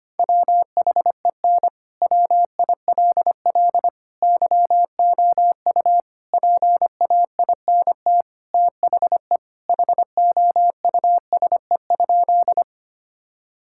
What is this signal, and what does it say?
Morse code, heard as a keyed tone, about 25 wpm, 700 Hz. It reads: WHEN WILL YOU PAINT THE HOUSE?
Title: Sentences from Top 1000 Words - Review Part 6 - 25wpm